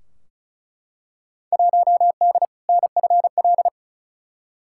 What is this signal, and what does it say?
Morse code, heard as a keyed tone, about 36 wpm, 700 Hz. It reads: 1D NFL
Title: Field Day Exchanges - 35wpm